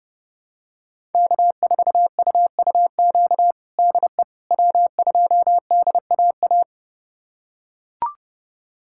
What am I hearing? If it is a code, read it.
Morse code, heard as a keyed tone, about 30 wpm, 700 Hz. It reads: K4UUQ DE W2DAA
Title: USA Callsign DE Callsign - 5 Characters - 30wpm